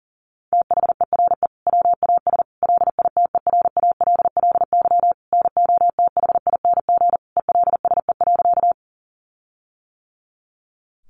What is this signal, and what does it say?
Morse code, heard as a keyed tone, about 40 wpm, 700 Hz. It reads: THERE WAS LITERALLY NOTHING ELSE.
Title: Sentences from Top 1400 Words 40wpm